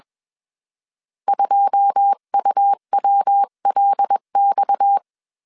Morse code, heard as a keyed tone, about 21 wpm, 800 Hz. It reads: 2UWLX
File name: CW / DET / 1 9 1500C S B 20wpm 2313